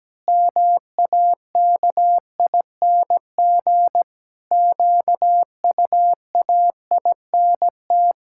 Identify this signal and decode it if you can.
Morse code, heard as a keyed tone, about 17 wpm, 700 Hz. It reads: MAKING QUAINT
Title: Pangram Challenge v2 - 17wpm